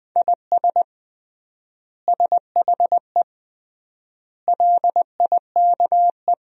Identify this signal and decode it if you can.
Morse code, heard as a keyed tone, about 20 wpm, 700 Hz. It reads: IS SHE LIKE
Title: 3x Word-Spacing - Sentences from Top 100 Words - 20wpm